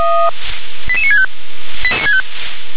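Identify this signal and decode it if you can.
Five-tone selective call (ZVEI), heard as quick runs of five tones, tones of about 70 ms. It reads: 88965
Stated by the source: ZVEI five-tone call received on SDR.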